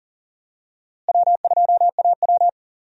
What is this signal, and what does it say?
Morse code, heard as a keyed tone, about 37 wpm, 700 Hz. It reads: W2AW
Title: USA Call Signs - 4 Characters - Repeated 3x - 40wpm